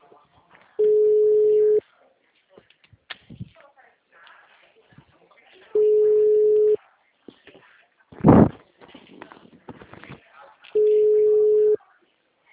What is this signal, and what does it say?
Ringback tone (Europe, 425 Hz), with speech.
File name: out- -2013- - - .8